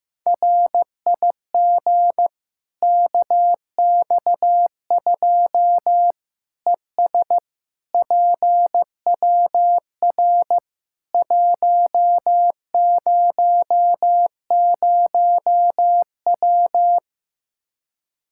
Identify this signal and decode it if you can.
Morse code, heard as a keyed tone, about 15 wpm, 700 Hz. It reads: RIG KX2 ES PWR 100W
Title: Practice QSO - Exchange 2 - Info - 15wpm